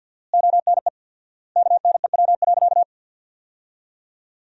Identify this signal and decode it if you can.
Morse code, heard as a keyed tone, about 47 wpm, 700 Hz. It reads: ONE KNEW.
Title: 4x Word-Spacing - Sentences from Top 400 Words - 50wpm